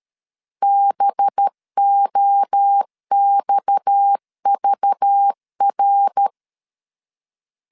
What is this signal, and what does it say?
Morse code, heard as a keyed tone, about 13 wpm, 800 Hz. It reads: BOXVR
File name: LEZIONI / 6 PKX0 / 6 PKX0 12wpm 2231